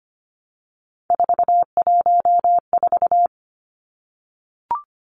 Morse code, heard as a keyed tone, about 25 wpm, 700 Hz. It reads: 414
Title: Three Challenging Letter-Number Combinations 25wpm